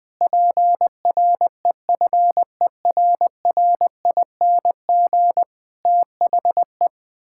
Morse code, heard as a keyed tone, about 20 wpm, 700 Hz. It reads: PREFERRING THE
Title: Pangram Challenge v2 - 20wpm